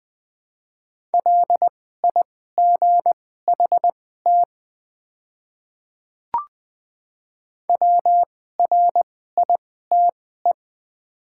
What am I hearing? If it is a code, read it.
Morse code, heard as a keyed tone, about 20 wpm, spaced as for 15 wpm, 700 Hz. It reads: LIGHT WRITE
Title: 2x Character Spacing - Top 200 Words - 20wpm